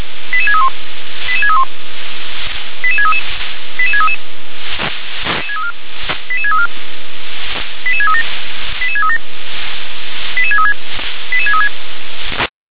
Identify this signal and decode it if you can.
Five-tone selective call (ZVEI), heard as quick runs of five tones, tones of about 70 ms. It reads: 88631 88631 88633 88633 88634 88637 88637 88636 88636